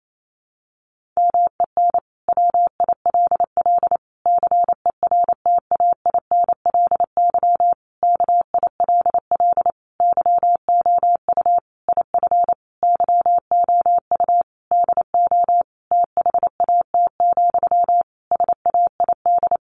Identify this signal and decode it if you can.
Morse code, heard as a keyed tone, about 28 wpm, 700 Hz. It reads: MEN WILL CERTAINLY KILL YOU IF YOU DO THAT, SAID
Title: Sentences from Top 1500 Words 28wpm